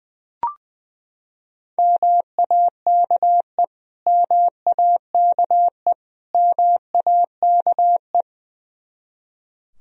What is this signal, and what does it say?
Morse code, heard as a keyed tone, about 20 wpm, 700 Hz. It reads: MAKE MAKE MAKE
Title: Top 100 Words Repeated 20wpm